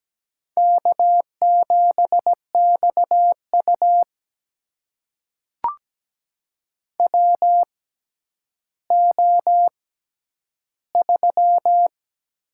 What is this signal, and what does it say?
Morse code, heard as a keyed tone, about 17 wpm, spaced as for 10 wpm, 700 Hz. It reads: K7XU WO3
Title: USA Call Signs - 6x Character Spacing - 17wpm